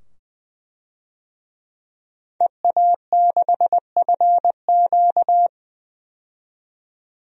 Morse code, heard as a keyed tone, about 20 wpm, 700 Hz. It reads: EA6FQ
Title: DXCC-100 Call Signs - 5 Characters - 20wpm